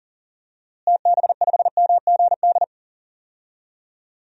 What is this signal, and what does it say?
Morse code, heard as a keyed tone, about 40 wpm, 700 Hz. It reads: TB5MGD
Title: DXCC-100 Call Signs - 6 Characters - Repeated 2x - 40wpm